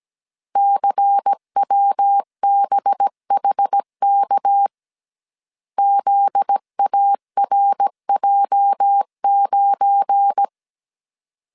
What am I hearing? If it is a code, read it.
Morse code, heard as a keyed tone, about 17 wpm, 800 Hz. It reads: CWBHK ZARJ9
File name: CW / DET / 1 9 1500C S A 16wpm 2828